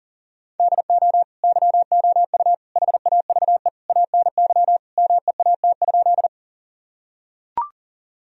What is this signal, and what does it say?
Morse code, heard as a keyed tone, about 40 wpm, 700 Hz. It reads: DO YOU HAVE ANY MEAT?
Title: Sentences from Top 1000 Words - Review Part 3 - 40wpm